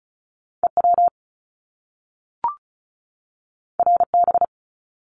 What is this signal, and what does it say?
Morse code, heard as a keyed tone, about 36 wpm, 700 Hz. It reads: EW RB
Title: Top Two Letter Combinations in Words 35wpm